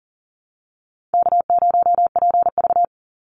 Morse code, heard as a keyed tone, about 40 wpm, 700 Hz. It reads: K0PV
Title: Easy Call Signs Repeated 3x - 40wpm